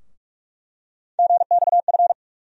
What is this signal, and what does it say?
Morse code, heard as a keyed tone, about 46 wpm, 700 Hz. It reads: GXF
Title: USA Call Sign Postfixes - 3 Letters 45wpm